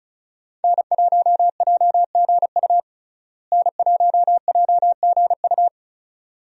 Morse code, heard as a keyed tone, about 34 wpm, 700 Hz. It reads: N1JGU N1JGU
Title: USA Call Signs - 5 Characters - Repeated 2x - 35wpm